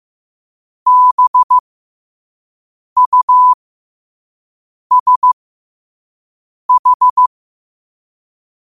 Morse code, heard as a keyed tone, about 15 wpm, spaced as for 5 wpm, 1000 Hz. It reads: BUSH